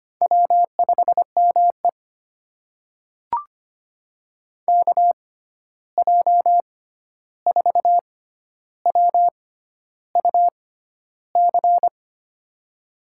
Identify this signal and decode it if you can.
Morse code, heard as a keyed tone, about 25 wpm, spaced as for 11 wpm, 700 Hz. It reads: W5ME KJ4WUC
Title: USA Call Signs - 6x Character Spacing - 25wpm